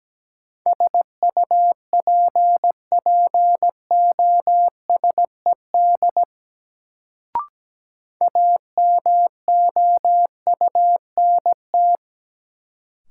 Morse code, heard as a keyed tone, about 17 wpm, 700 Hz. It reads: SUPPOSED AMOUNT
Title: Sets of 3 Words - New Word Drill - 17wpm